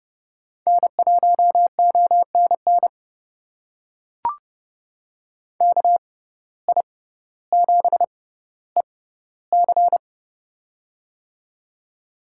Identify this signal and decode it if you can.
Morse code, heard as a keyed tone, about 30 wpm, spaced as for 15 wpm, 700 Hz. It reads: N1ONN KI7EC
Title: USA Call Signs - 5 Characters - 6x Character Spacing - 30wpm